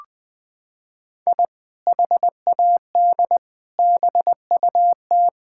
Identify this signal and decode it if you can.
Morse code, heard as a keyed tone, about 20 wpm, 700 Hz. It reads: I HAD BUT
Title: Sentences from Top 1000 Words - Review Part 6 - 20wpm